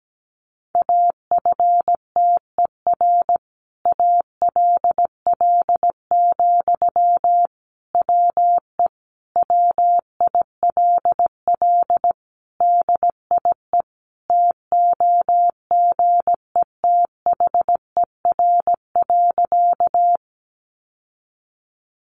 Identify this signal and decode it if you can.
Morse code, heard as a keyed tone, about 17 wpm, 700 Hz. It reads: AFTER ALL, WE WILL DIE TOGETHER.
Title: Sentences from Top 1300 Words 17wpm